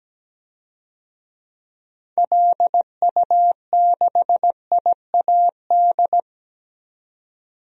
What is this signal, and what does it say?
Morse code, heard as a keyed tone, about 17 wpm, 700 Hz. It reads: LU6IAD
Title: DXCC-100 Call Signs - 6 Characters - 17wpm